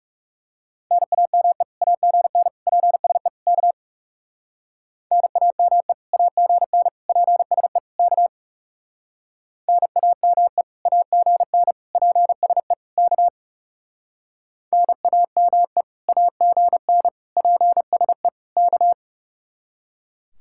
Morse code, heard as a keyed tone, about 35 wpm, 700 Hz. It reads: NAME AGN PSE K NAME AGN PSE K NAME AGN PSE K NAME AGN PSE K
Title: Speed-Racing Common QSO Phrases - 30wpm